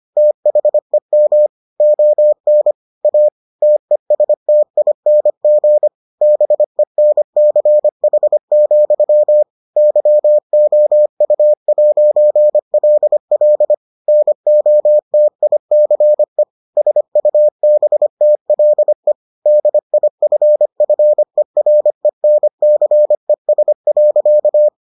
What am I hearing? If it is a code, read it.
Morse code, heard as a keyed tone, about 25 wpm, 600 Hz. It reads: THEM ON A TESTING BENCH, YOU'LL NOTICE SUBTLE DIFFERENCES.